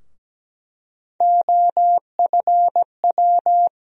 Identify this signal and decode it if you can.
Morse code, heard as a keyed tone, about 17 wpm, 700 Hz. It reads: OFW
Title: USA Call Sign Postfixes - 3 Letters 17wpm